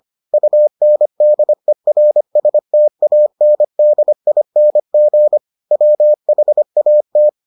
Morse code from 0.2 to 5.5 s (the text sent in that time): UNDERSTANDING